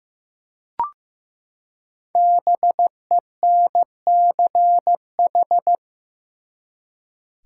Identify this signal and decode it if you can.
Morse code, heard as a keyed tone, about 15 wpm, 700 Hz. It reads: BENCH
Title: Five Letter Words - 15wpm